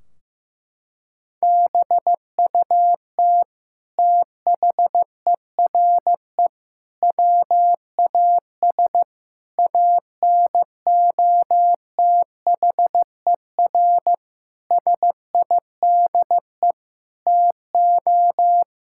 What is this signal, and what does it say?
Morse code, heard as a keyed tone, about 15 wpm, 700 Hz. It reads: BUT THERE WAS ANOTHER SIDE TO
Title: Sentences from Top 1600 Words 15wpm